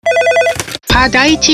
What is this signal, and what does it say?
Speech. There is no code.